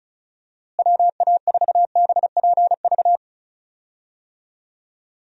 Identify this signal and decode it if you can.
Morse code, heard as a keyed tone, about 34 wpm, 700 Hz. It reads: WA4BPV
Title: USA Call Signs - 6 Characters - Repeated 3x - 35wpm